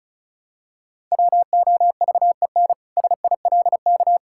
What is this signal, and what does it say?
Morse code, heard as a keyed tone, about 36 wpm, 700 Hz. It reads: WOVEN SILK
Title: Pangram Challenge v2 - 35wpm